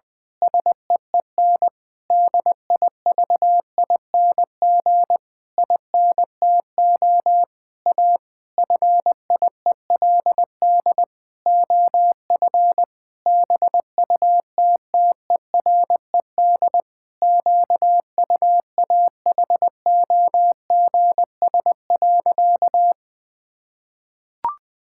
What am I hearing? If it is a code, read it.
Morse code, heard as a keyed tone, about 21 wpm, 700 Hz. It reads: SEEN DIVING INTO A FIELD OF BUTTERED QUAHOGS.
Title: Pangram Challenge v2 - 20wpm